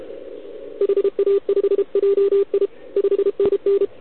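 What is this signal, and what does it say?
Morse code, heard as a keyed tone, about 30 wpm, 400 Hz, with radio noise. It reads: HA5JI 5NN
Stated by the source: CW from the OKOM DX contest on 80 m, received with a TS480.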